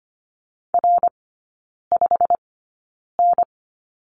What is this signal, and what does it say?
Morse code, heard as a keyed tone, about 25 wpm, spaced as for 9 wpm, 690 Hz. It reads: R5N
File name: Lesson 15e - Number 5 - Three Characters 25wpm